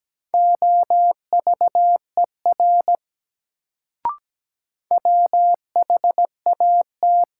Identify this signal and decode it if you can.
Morse code, heard as a keyed tone, about 17 wpm, 700 Hz. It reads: OVER WHAT
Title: Sets of 2 Words - New Word Drill - 17wpm